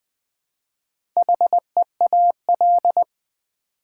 Morse code, heard as a keyed tone, about 20 wpm, 700 Hz. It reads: HEAL
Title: Build A Word - 1 Letter At A Time - 4-Letter Words - 20wpm